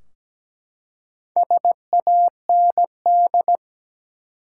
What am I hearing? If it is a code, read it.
Morse code, heard as a keyed tone, about 17 wpm, 700 Hz. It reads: SAND
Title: Four Letter Words - 17wpm